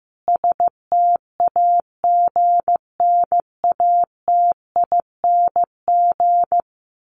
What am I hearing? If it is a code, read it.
Morse code, heard as a keyed tone, about 14 wpm, 700 Hz. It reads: STAGNATING
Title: A Sound Of Thunder 15wpm